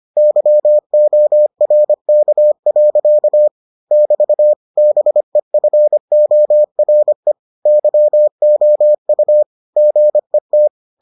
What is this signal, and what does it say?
Morse code, heard as a keyed tone, about 25 wpm, 600 Hz. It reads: YORK. <BT> BEFORE YOU GET